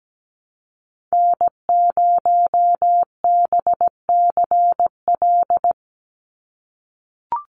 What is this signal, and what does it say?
Morse code, heard as a keyed tone, about 17 wpm, 700 Hz. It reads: N0BCL
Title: USA Call Signs 17wpm